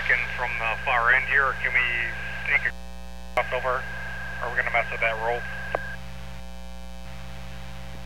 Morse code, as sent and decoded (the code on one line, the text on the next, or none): none